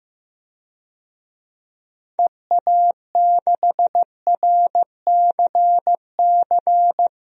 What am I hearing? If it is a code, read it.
Morse code, heard as a keyed tone, about 15 wpm, 700 Hz. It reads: EA6RCC
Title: DXCC-100 Call Signs - 6 Characters - Repeated 3x - 15wpm